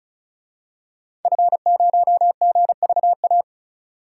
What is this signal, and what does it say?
Morse code, heard as a keyed tone, about 36 wpm, 700 Hz. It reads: F0GVA
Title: DXCC-100 Call Signs - 5 Characters - 35wpm